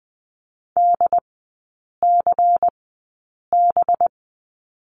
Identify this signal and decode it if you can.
Morse code, heard as a keyed tone, about 20 wpm, spaced as for 8 wpm, 700 Hz. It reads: DCB
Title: 205 Lesson 33e - Letter j - Three Characters 20wpm